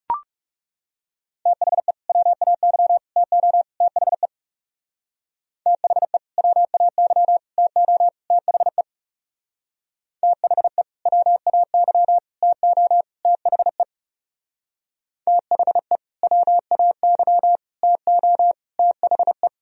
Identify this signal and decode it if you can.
Morse code, heard as a keyed tone, about 35 wpm, 700 Hz. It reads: THE WAY TO THE THE WAY TO THE THE WAY TO THE THE WAY TO THE
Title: Speed-Racing - Sets of 4 Words 30wpm